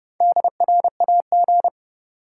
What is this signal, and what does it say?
Morse code, heard as a keyed tone, about 30 wpm, 700 Hz. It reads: DRAG
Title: Sentences from Top 1500 Words - New Word Drill - 30wpm